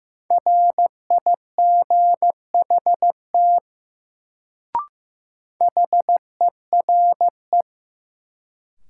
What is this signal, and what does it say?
Morse code, heard as a keyed tone, about 15 wpm, 700 Hz. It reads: RIGHT HERE
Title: Sentences from Top 200 Words - New Word Drill - 15wpm